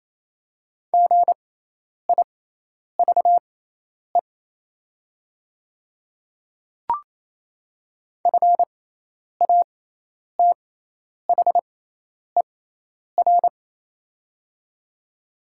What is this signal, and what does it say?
Morse code, heard as a keyed tone, about 28 wpm, spaced as for 10 wpm, 700 Hz. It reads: GIVE FATHER
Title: 6x Character Spacing - Top 200 Words - 28wpm